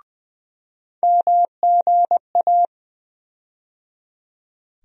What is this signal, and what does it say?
Morse code, heard as a keyed tone, about 20 wpm, 700 Hz. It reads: MGA
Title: USA Call Sign Postfixes - 3 Letters 20wpm